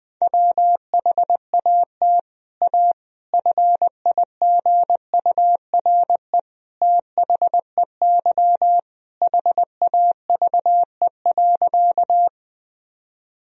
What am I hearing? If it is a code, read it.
Morse code, heard as a keyed tone, about 20 wpm, 700 Hz. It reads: WHAT A FIGURE THEY HAVE.
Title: Sentences from Top 1000 Words - Review Part 2 - 20wpm